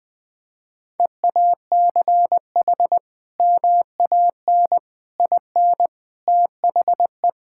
Morse code, heard as a keyed tone, about 20 wpm, 700 Hz. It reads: EACH MAN IN THE
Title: Sentences from Top 1600 Words 20wpm